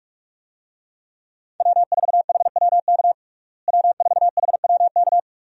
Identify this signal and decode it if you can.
Morse code, heard as a keyed tone, about 45 wpm, 700 Hz. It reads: W4HWK W4HWK
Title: USA Call Signs - 5 Characters - Repeated 2x - 45wpm